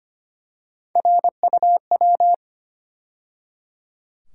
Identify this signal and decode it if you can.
Morse code, heard as a keyed tone, about 25 wpm, 700 Hz. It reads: RUW
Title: USA Call Sign Postfixes - 3 Letters 25wpm